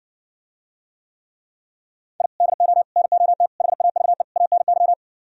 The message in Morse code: I DO NOT HAVE ANY

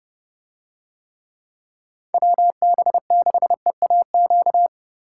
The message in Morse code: WB6EAQ

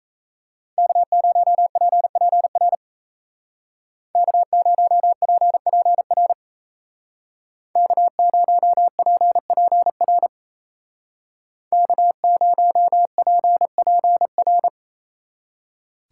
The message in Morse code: K0PPR K0PPR K0PPR K0PPR

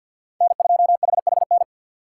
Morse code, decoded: N2HHN

